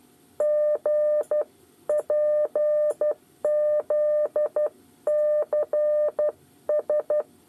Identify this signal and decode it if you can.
Morse code, heard as a keyed tone, about 12 wpm, 600 Hz. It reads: G PZCS
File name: 8 bis lezione cw 2